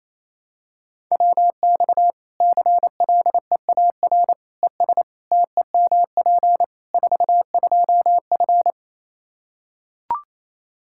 Morse code, heard as a keyed tone, about 27 wpm, 700 Hz. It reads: WX CLEAR ES TEMP 42F
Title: Practice QSO - Exchange 2 - Info - 28wpm